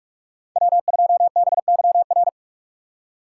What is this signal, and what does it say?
Morse code, heard as a keyed tone, about 42 wpm, 700 Hz. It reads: W2BYR